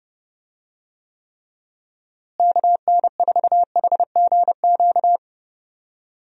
Morse code, heard as a keyed tone, about 29 wpm, 700 Hz. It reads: KN4HGQ
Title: USA Call Signs - 6 Characters - Repeated 3x - 30wpm